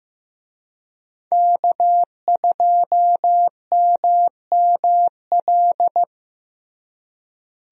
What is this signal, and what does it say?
Morse code, heard as a keyed tone, about 15 wpm, 700 Hz. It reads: K2MML
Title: USA Call Signs - 5 Characters - Repeated 3x - 15wpm